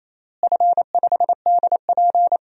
Morse code, heard as a keyed tone, about 28 wpm, 700 Hz. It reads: F5DP
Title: DXCC-100 Call Signs - 4 Characters - Repeated 3x - 28wpm